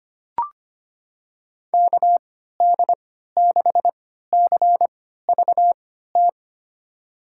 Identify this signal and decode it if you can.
Morse code, heard as a keyed tone, about 25 wpm, spaced as for 14 wpm, 700 Hz. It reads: KD6CVT
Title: USA Call Signs - 6 Characters - 3x Character Spacing - 25wpm